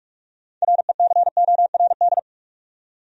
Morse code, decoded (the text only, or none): RECORD